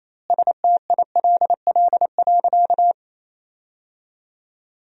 Morse code, heard as a keyed tone, about 28 wpm, 700 Hz. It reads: STILL.
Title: 2x Word-Spacing - Sentences from Top 500 Words - 28wpm